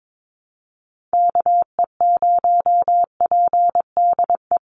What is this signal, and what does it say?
Morse code, heard as a keyed tone, about 22 wpm, 700 Hz. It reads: KE0PDE
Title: USA Call Signs 22wpm